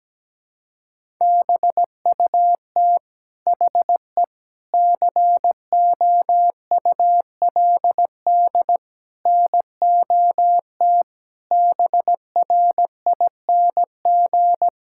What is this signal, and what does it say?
Morse code, heard as a keyed tone, about 17 wpm, 700 Hz. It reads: BUT HE COULD NOT BRING
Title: Sentences from Top 1600 Words 17wpm